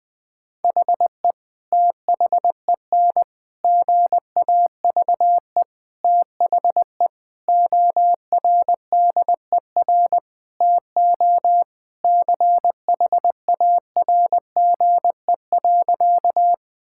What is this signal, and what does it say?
Morse code, heard as a keyed tone, about 20 wpm, 700 Hz. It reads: HE THEN GAVE THE ORDER TO CHARGE.
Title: Sentences from Top 1000 Words - Review Part 6 - 20wpm